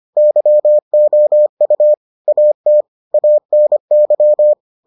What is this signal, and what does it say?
Morse code, heard as a keyed tone, about 25 wpm, 600 Hz. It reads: YOU AT ANY